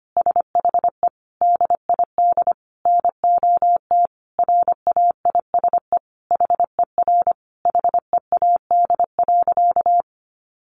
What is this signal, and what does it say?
Morse code, heard as a keyed tone, about 25 wpm, 700 Hz. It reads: SHE DID NOT RAISE HER HEAD.
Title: Sentences from Top 1400 Words 25wpm